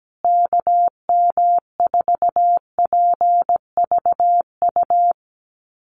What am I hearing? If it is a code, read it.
Morse code, heard as a keyed tone, about 17 wpm, 700 Hz. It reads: KM4PVU
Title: USA Call Signs Repeated 3x 17wpm